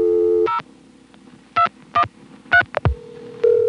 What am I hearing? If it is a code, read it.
DTMF, heard as telephone touch tones, tones of about 100 ms. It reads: 0213